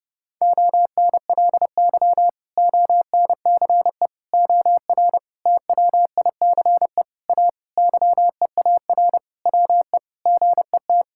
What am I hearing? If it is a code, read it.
Morse code, heard as a keyed tone, about 32 wpm, 700 Hz. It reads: ONLY ONCE OR TWICE A YEAR WE GET
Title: Sentences from Top 1600 Words 30wpm